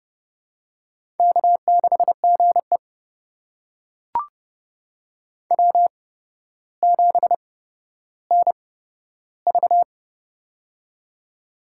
Morse code, heard as a keyed tone, about 30 wpm, spaced as for 13 wpm, 700 Hz. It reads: K6GE W7NV
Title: USA Call Signs - 4 Characters - 8x Character Spacing - 30wpm